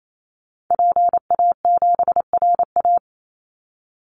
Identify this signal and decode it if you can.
Morse code, heard as a keyed tone, about 28 wpm, 700 Hz. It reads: PA7RA
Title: Call Signs Repeated 3x - 28wpm